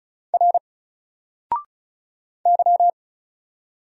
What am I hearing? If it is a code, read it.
Morse code, heard as a keyed tone, about 33 wpm, 700 Hz. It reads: R Y
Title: Single Character (Letter:Number:Punctuation:Prosign) Alt 35wpm